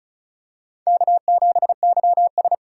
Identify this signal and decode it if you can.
Morse code, heard as a keyed tone, about 36 wpm, 700 Hz. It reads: K7YS
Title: USA Call Signs - 2x Character Spacing - 35wpm